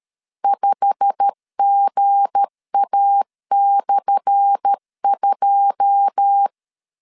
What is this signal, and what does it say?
Morse code, heard as a keyed tone, about 13 wpm, 800 Hz. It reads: 5GA/2
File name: LEZIONI / 10 SIGNS / 10A SLASH 12wpm 2314